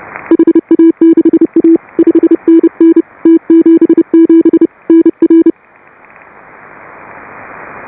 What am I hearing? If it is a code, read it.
Morse code, heard as a keyed tone, about 31 wpm, 310 Hz. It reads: HA6A 5NN T77 NR
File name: AUDIO2018 / HA5BA CW